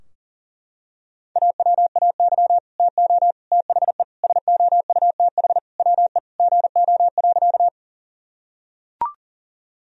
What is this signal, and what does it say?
Morse code, heard as a keyed tone, about 37 wpm, 700 Hz. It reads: AWAY TO THE SOUTH WE GO.
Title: Sentences from Top 1000 Words - Review Part 2 - 40wpm